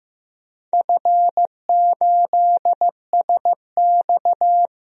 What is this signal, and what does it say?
Morse code, heard as a keyed tone, about 15 wpm, 700 Hz. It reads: F8SX
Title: DXCC-100 Call Signs - 4 Characters - 15wpm